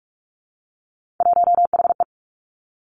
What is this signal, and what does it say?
Morse code, heard as a keyed tone, about 46 wpm, 700 Hz. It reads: 1HE
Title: Three Challenging Letter-Number Combinations 45wpm